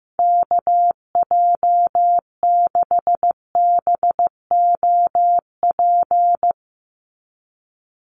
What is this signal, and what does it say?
Morse code, heard as a keyed tone, about 15 wpm, 700 Hz. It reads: KJ6BOP
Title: USA Call Signs 15wpm